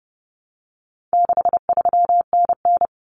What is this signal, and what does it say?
Morse code, heard as a keyed tone, about 30 wpm, 700 Hz. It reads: 63NN
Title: Common Cut Numbers - 4-Digit 30wpm